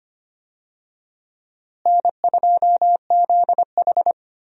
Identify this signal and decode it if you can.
Morse code, heard as a keyed tone, about 25 wpm, 700 Hz. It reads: N2ZH
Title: USA Call Signs - 4 Characters - Repeated 3x - 25wpm